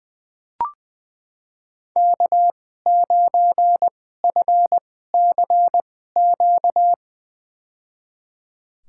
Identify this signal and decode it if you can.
Morse code, heard as a keyed tone, about 20 wpm, spaced as for 15 wpm, 700 Hz. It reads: K9FCQ
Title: USA Call Signs - 5 Characters - 2x Character Spacing - 20wpm